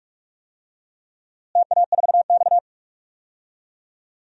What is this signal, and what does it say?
Morse code, heard as a keyed tone, about 42 wpm, 700 Hz. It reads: TA4X
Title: DXCC-100 Call Signs - 4 Characters - Repeated 3x - 45wpm